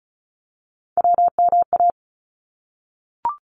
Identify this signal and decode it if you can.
Morse code, heard as a keyed tone, about 36 wpm, 700 Hz. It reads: WMA
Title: ARRL Sections 35wpm